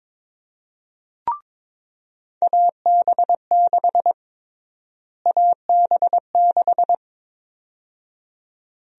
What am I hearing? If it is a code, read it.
Morse code, heard as a keyed tone, about 22 wpm, 700 Hz. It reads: AB6 AB6
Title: USA Call Sign Prefixes - Repeated 2x - 22wpm